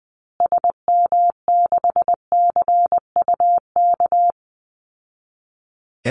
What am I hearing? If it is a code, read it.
Morse code, heard as a keyed tone, about 20 wpm, 700 Hz. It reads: SM6CUK
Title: Call Signs Repeated 2x - 20wpm